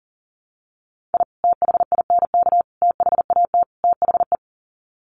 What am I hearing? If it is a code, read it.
Morse code, heard as a keyed tone, about 37 wpm, 700 Hz. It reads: I THINK THAT THE
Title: Sets of 4 Words 40wpm